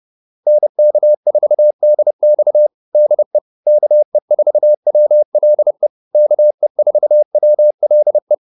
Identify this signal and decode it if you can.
Morse code, heard as a keyed tone, about 30 wpm, 600 Hz. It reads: NK4DX DE KE4WLE KE4WLE